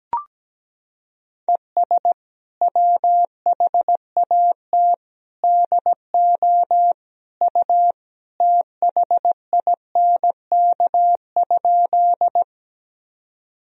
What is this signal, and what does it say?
Morse code, heard as a keyed tone, about 17 wpm, 700 Hz. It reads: ES WHAT DO U THINK?
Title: Sentences from Top 200 Words and Abbreviations 17wpm